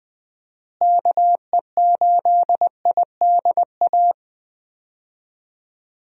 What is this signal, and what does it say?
Morse code, heard as a keyed tone, about 20 wpm, 700 Hz. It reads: KE8IDA
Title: USA Call Signs - 6 Characters - Repeated 3x - 20wpm